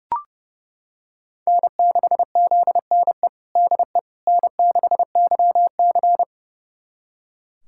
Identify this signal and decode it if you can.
Morse code, heard as a keyed tone, about 30 wpm, 700 Hz. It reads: N6ZNE DE N6YC
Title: USA Callsign DE Callsign 30wpm